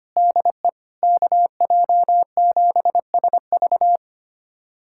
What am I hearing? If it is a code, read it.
Morse code, heard as a keyed tone, about 25 wpm, 700 Hz. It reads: DE KJ7SV
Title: USA Callsign DE Callsign - 5 Characters - 25wpm